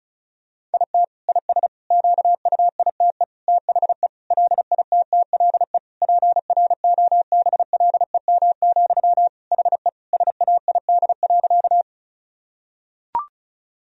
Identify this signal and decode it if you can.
Morse code, heard as a keyed tone, about 36 wpm, 700 Hz. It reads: IT IS QUITE THE LITTLE PROBLEM, HE SAID.
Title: Sentences from Top 1000 Words - Review Part 2 - 35wpm